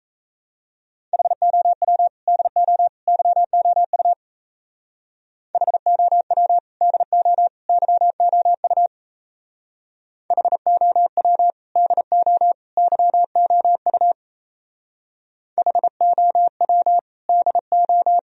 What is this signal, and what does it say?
Morse code, heard as a keyed tone, about 33 wpm, 700 Hz. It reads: HOW DO YOU HOW DO YOU HOW DO YOU HOW DO
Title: Speed-Racing - Sets of 3 Words 28wpm